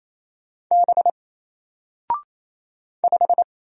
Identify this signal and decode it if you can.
Morse code, heard as a keyed tone, about 28 wpm, 700 Hz. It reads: B 5
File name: Single-Letter-Number-28wpm